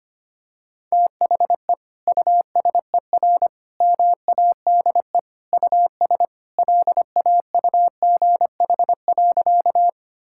Morse code, heard as a keyed tone, about 25 wpm, 700 Hz. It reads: THE USER MADE US LAUGH.
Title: Sentences from Top 1000 Words - Review Part 5 - 25wpm